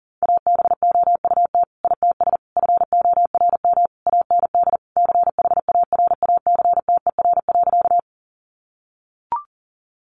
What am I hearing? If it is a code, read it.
Morse code, heard as a keyed tone, about 40 wpm, 700 Hz. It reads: ABOUT ITS FORM AND CHARACTER.
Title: Sentences from Top 1200 Words 40wpm